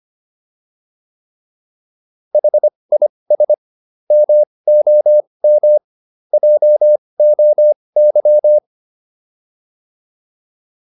Morse code, heard as a keyed tone, about 25 wpm, spaced as for 20 wpm, 600 Hz. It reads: HIS MOM JOY